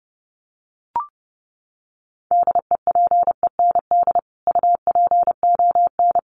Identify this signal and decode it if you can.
Morse code, heard as a keyed tone, about 30 wpm, 700 Hz. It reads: DEPEND UPON
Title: Sentences from Top 1300 Words 30wpm